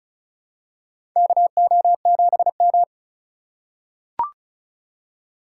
Morse code, heard as a keyed tone, about 34 wpm, 700 Hz. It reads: KO7M
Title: USA Call Signs - 4 Characters - 2x Character Spacing - 35wpm